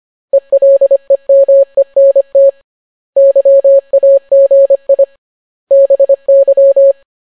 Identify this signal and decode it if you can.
Morse code, heard as a keyed tone, about 25 wpm, 550 Hz. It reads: ELEMENT YAGI BY